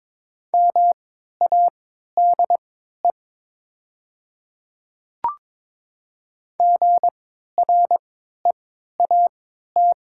Morse code, heard as a keyed tone, about 22 wpm, spaced as for 12 wpm, 700 Hz. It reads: MADE GREAT
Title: 3x Character Spacing - Top 200 Words - 22wpm